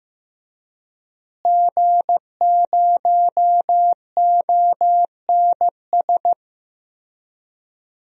Morse code, heard as a keyed tone, about 15 wpm, 700 Hz. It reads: G0ONS
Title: DXCC-100 Call Signs - 5 Characters - Repeated 3x - 15wpm